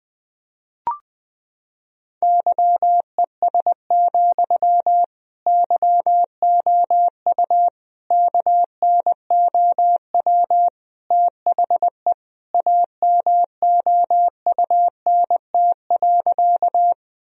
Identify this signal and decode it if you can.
Morse code, heard as a keyed tone, about 21 wpm, 700 Hz. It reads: YES, YOU KNOW THE AMOUNT.
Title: Sentences from Top 1000 Words - Review Part 4 - 20wpm